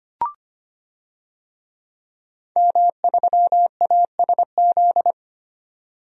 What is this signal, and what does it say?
Morse code, heard as a keyed tone, about 25 wpm, 700 Hz. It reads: M3ASZ
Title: DXCC-100 Call Signs - 5 Characters - 25wpm